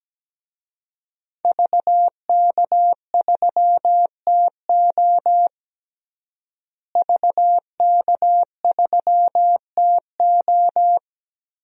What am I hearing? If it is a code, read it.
Morse code, heard as a keyed tone, about 17 wpm, 700 Hz. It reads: VK3TO VK3TO
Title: DXCC-100 Call Signs - 5 Characters - Repeated 3x - 17wpm